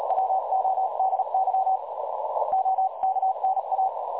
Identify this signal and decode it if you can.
Morse code, heard as a keyed tone, about 37 wpm, 760 Hz, with noise. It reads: WODY OKNA
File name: dlazaawansowanych / nagraniazpasma / websdr recording 3565.0kHz